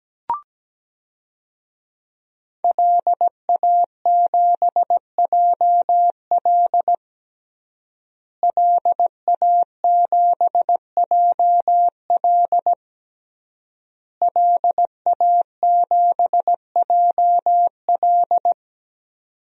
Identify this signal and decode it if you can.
Morse code, heard as a keyed tone, about 17 wpm, 700 Hz. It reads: LA7JL LA7JL LA7JL
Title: DXCC-100 Call Signs - 5 Characters - Repeated 3x - 17wpm